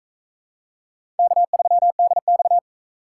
Morse code, heard as a keyed tone, about 36 wpm, 700 Hz. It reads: K3DX